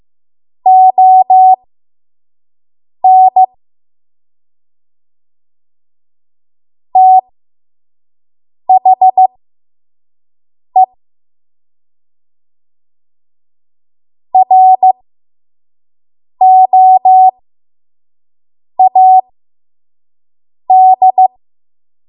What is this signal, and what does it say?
Morse code, heard as a keyed tone, about 15 wpm, spaced as for 5 wpm, 750 Hz. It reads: ON THE ROAD
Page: 5 WPM Code Archive